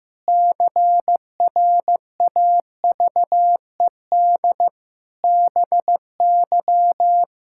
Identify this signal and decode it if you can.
Morse code, heard as a keyed tone, about 15 wpm, 700 Hz. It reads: CRAVED BY
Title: Pangram Challenge v2 - 15wpm